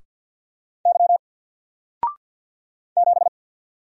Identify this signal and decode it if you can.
Morse code, heard as a keyed tone, about 49 wpm, 700 Hz. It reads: Y 7
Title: Single Character (Letter:Number:Punctuation:Prosign) Alt 50wpm